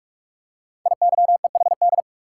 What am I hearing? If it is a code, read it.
Morse code, heard as a keyed tone, about 44 wpm, 700 Hz. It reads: IYEHD
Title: Code Groups - 5 Letters - 45wpm